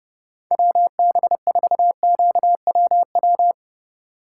Morse code, heard as a keyed tone, about 35 wpm, 700 Hz. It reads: WB4QWW
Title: Speed Racing - USA 2x3 Call Signs 30wpm